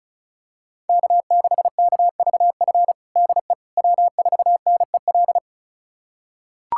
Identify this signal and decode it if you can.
Morse code, heard as a keyed tone, about 36 wpm, 700 Hz. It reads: K6KVF DE W4NEL
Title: USA Callsign DE Callsign - 5 Characters - 35wpm